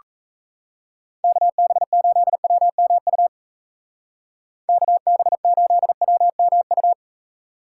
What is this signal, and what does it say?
Morse code, heard as a keyed tone, about 34 wpm, 700 Hz. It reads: KB8WMU KB8WMU
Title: Speed Racing - USA 2x3 Call Signs 28wpm